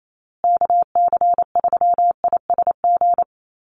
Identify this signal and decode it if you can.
Morse code, heard as a keyed tone, about 28 wpm, 700 Hz. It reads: KC3ISG
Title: USA Call Signs Repeated 3x 28wpm